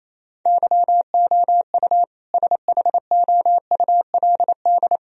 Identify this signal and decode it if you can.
Morse code, heard as a keyed tone, about 28 wpm, 700 Hz. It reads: YOU SHOULD
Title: Sentences from Top 1000 Words - Review Part 1 - 28wpm